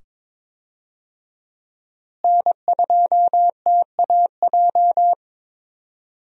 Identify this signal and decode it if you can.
Morse code, heard as a keyed tone, about 22 wpm, 700 Hz. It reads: N2TAJ
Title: USA Call Signs - 5 Characters - Repeated 2x - 22wpm